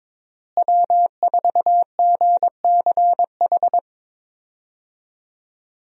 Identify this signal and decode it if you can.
Morse code, heard as a keyed tone, about 22 wpm, 700 Hz. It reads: W4GCH
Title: USA Call Signs - 5 Characters - Repeated 3x - 22wpm